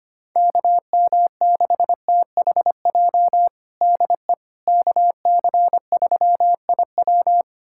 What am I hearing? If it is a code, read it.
Morse code, heard as a keyed tone, about 25 wpm, 700 Hz. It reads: KM6THJ DE KC3IW
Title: USA Callsign DE Callsign 25wpm